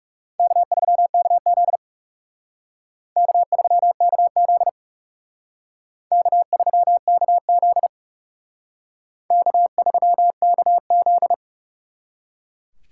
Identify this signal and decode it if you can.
Morse code, heard as a keyed tone, about 36 wpm, 700 Hz. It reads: K3KZ K3KZ K3KZ K3KZ
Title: Speed Racing - USA Call Signs 30wpm